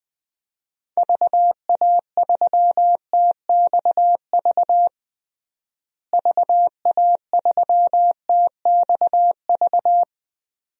Morse code, heard as a keyed tone, about 21 wpm, 700 Hz. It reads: VA3TXV VA3TXV
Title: DXCC-100 Call Signs - 6 Characters - Repeated 2x - 20wpm